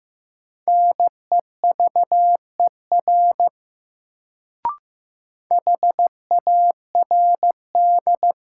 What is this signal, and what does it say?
Morse code, heard as a keyed tone, about 15 wpm, 700 Hz. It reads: NEVER HARD
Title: Sentences from Top 300 Words - New Word Drill - 15wpm